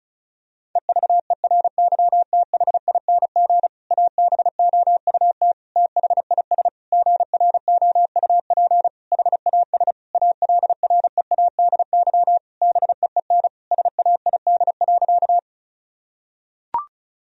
Morse code, heard as a keyed tone, about 34 wpm, 700 Hz. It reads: EVERYTHING ABOUT THIS GROUP HAS ALREADY BEEN SAID.
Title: Sentences from Top 1000 Words - Review Part 1 - 35wpm